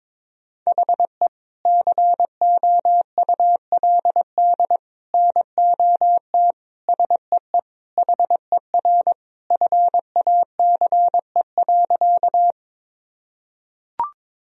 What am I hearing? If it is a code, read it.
Morse code, heard as a keyed tone, about 22 wpm, 700 Hz. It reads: HE COULD NOT SEE HER FACE.
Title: Sentences from Top 1000 Words - Review Part 2 - 22wpm